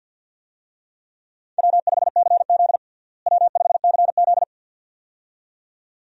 Morse code, heard as a keyed tone, about 52 wpm, 700 Hz. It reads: W5CZ W5CZ
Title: USA Call Signs - 4 Characters - Repeated 2x - 50wpm